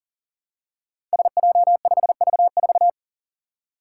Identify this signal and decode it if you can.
Morse code, heard as a keyed tone, about 39 wpm, 700 Hz. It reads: SJ5V4
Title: Five Challenging Letter-Number Combinations 40wpm